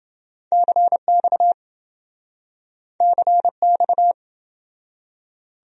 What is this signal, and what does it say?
Morse code, heard as a keyed tone, about 27 wpm, 700 Hz. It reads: CX CX